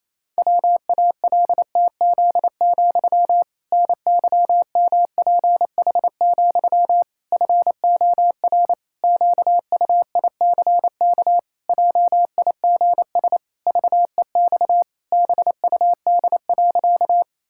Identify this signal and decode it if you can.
Morse code, heard as a keyed tone, about 27 wpm, 700 Hz. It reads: WALTZ, NYMPH, FOR QUICK JIGS VEX BUD.
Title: Pangram Challenge v2 - 28wpm